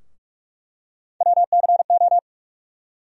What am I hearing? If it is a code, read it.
Morse code, heard as a keyed tone, about 42 wpm, 700 Hz. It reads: WCO
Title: USA Call Sign Postfixes - 3 Letters 45wpm